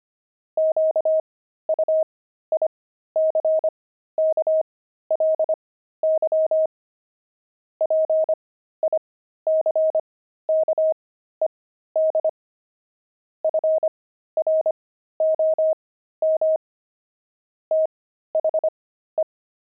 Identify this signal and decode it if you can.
Morse code, heard as a keyed tone, about 25 wpm, spaced as for 13 wpm, 630 Hz. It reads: QUICKLY PICKED FROM THE
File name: pangram-fs-13-cs-25-sixty